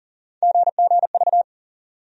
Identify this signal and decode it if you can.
Morse code, heard as a keyed tone, about 40 wpm, 700 Hz. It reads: GGV
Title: USA Call Sign Postfixes - 3 Letters 40wpm